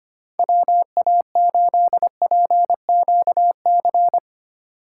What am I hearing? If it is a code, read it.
Morse code, heard as a keyed tone, about 28 wpm, 700 Hz. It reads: WA8PQC
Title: Speed Racing - USA 2x3 Call Signs 25wpm